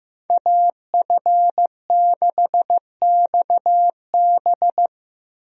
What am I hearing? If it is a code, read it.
Morse code, heard as a keyed tone, about 16 wpm, 700 Hz. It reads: AF6XB
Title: USA Call Signs - 5 Characters - 15wpm